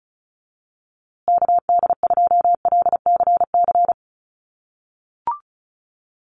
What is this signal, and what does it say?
Morse code, heard as a keyed tone, about 34 wpm, 700 Hz. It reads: KD2LCC
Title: USA Call Signs Repeated 2x 35wpm